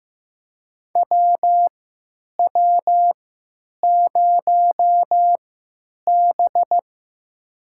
Morse code, heard as a keyed tone, about 15 wpm, spaced as for 9 wpm, 700 Hz. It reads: WW0B
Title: USA Call Signs - 4 Characters - 3x Character Spacing - 15wpm